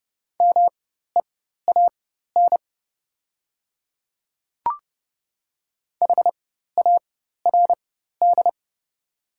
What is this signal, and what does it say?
Morse code, heard as a keyed tone, about 29 wpm, spaced as for 14 wpm, 700 Hz. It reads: MEAN HARD
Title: 4x Character Spacing - Top 300 Words - 30wpm